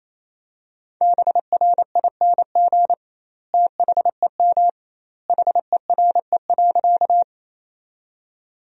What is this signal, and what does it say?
Morse code, heard as a keyed tone, about 27 wpm, 700 Hz. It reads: BRING THEM HERE.
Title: 2x Word-Spacing - Sentences from Top 500 Words - 28wpm